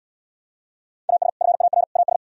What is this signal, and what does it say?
Morse code, heard as a keyed tone, about 74 wpm, 700 Hz. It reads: AS FAR AS